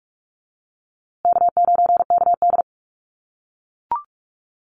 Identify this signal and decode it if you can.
Morse code, heard as a keyed tone, about 44 wpm, 690 Hz. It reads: K9KD
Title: USA Call Signs Repeated 3x 45wpm